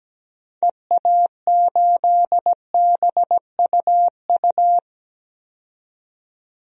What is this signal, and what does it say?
Morse code, heard as a keyed tone, about 17 wpm, 700 Hz. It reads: EA8BUU
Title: DXCC-100 Call Signs - 6 Characters - 17wpm